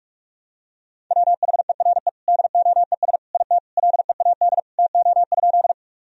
Morse code, heard as a keyed tone, about 42 wpm, 700 Hz. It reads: WHERE DOES IT LEAD TO?
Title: Sentences from Top 1000 Words - Review Part 2 - 45wpm